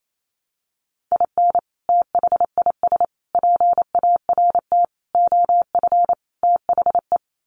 Morse code, heard as a keyed tone, about 28 wpm, 700 Hz. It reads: IN THIS PART OF THE
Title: Sets of 5 Words 28wpm